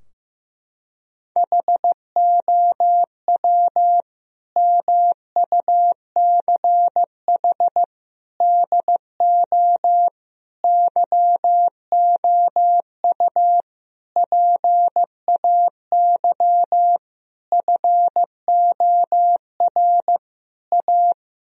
Morse code, heard as a keyed tone, about 15 wpm, 700 Hz. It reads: HOW MUCH DO YOU PAY FOR A
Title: Sentences from Top 1000 Words - Review Part 3 - 15wpm